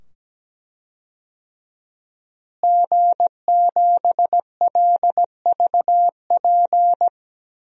Morse code, heard as a keyed tone, about 17 wpm, 700 Hz. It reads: G7LVP
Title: DXCC-100 Call Signs - 5 Characters - Repeated 2x - 17wpm